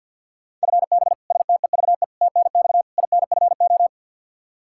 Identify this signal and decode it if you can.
Morse code, heard as a keyed tone, about 51 wpm, 700 Hz. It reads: FB STEVE TNX INFO
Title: Practice QSO - Exchange 3 - More Info - 50wpm